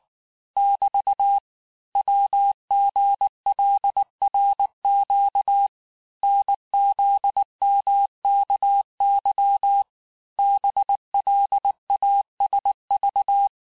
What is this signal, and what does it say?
Morse code, heard as a keyed tone, about 19 wpm, 800 Hz. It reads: <BT> WGLRQ NZMKY BLASV
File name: cours-du-mercredi / audios / cours-869-texte-CHIFFRE-vitesse-18-MPM